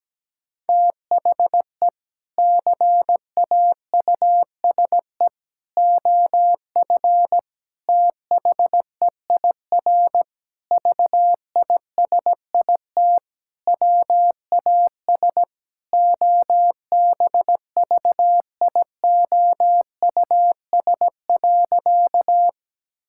Morse code, heard as a keyed tone, about 17 wpm, 700 Hz. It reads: THE CAUSE OF THEIR VISIT WAS OBVIOUS.
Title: Sentences from Top 1600 Words 17wpm